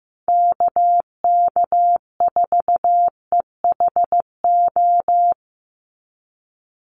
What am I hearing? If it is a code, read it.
Morse code, heard as a keyed tone, about 15 wpm, 700 Hz. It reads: KK4EHO
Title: USA Call Signs 15wpm